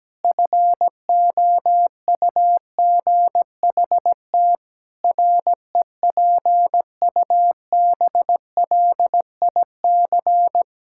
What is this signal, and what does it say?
Morse code, heard as a keyed tone, about 17 wpm, 700 Hz. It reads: FOUGHT REPUBLIC
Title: Pangram Challenge v2 - 17wpm